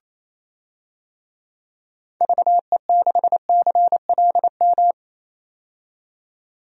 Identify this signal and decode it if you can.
Morse code, heard as a keyed tone, about 28 wpm, 700 Hz. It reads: VE6CLM
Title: DXCC-100 Call Signs - 6 Characters - 28wpm